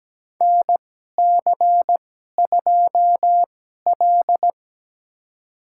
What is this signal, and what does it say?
Morse code, heard as a keyed tone, about 17 wpm, spaced as for 12 wpm, 700 Hz. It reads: NC2L
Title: USA Call Signs - 2x Character Spacing - 17wpm